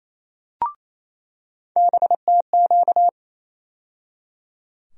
Morse code, heard as a keyed tone, about 27 wpm, 700 Hz. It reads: BTQ
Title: USA Call Sign Postfixes - 3 Letters 28wpm